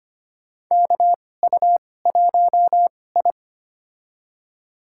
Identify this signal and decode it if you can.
Morse code, heard as a keyed tone, about 25 wpm, spaced as for 18 wpm, 700 Hz. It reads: KU1I
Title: USA Call Signs - 4 Characters - 2x Character Spacing - 25wpm